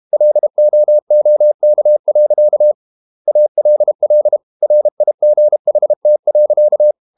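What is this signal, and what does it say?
Morse code, heard as a keyed tone, about 32 wpm, 600 Hz. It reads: LOOK. ALL RIGHT.
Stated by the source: Morse code practice at 32 wpm from CW med Gnister, reading Ken Follett's novel Never in English.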